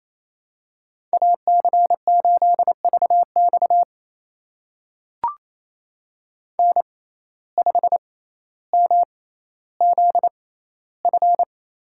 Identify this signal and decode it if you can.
Morse code, heard as a keyed tone, about 27 wpm, spaced as for 14 wpm, 700 Hz. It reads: AC8VX N5MZF
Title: USA Call Signs - 5 Characters - 6x Character Spacing - 28wpm